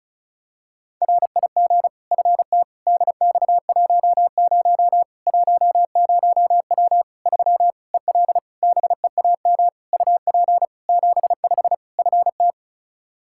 Morse code, heard as a keyed tone, about 34 wpm, 700 Hz. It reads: RIG FT DX10 10W 3 EL BEAM UP 75 FT
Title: K1USN SST QSO Practice 35wpm